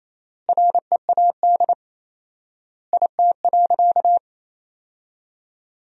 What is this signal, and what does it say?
Morse code, heard as a keyed tone, about 28 wpm, 700 Hz. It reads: READ IT.
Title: 4x Word-Spacing - Sentences from Top 500 Words - 28wpm